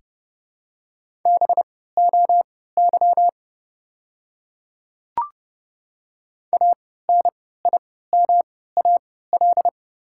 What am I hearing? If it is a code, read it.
Morse code, heard as a keyed tone, about 30 wpm, spaced as for 17 wpm, 700 Hz. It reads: BOY ANIMAL
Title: 3x Character Spacing - Top 300 Words - 30wpm